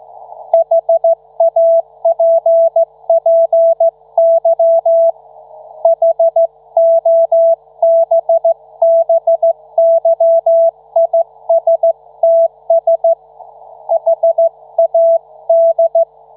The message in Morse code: HAPPY HOBBYISTS HAD